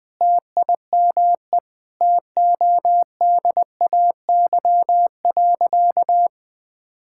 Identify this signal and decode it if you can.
Morse code, heard as a keyed tone, about 20 wpm, 700 Hz. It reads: TIME TODAY.
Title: Sentences with 4-Word Phrases - Spoken Beginning - 20wpm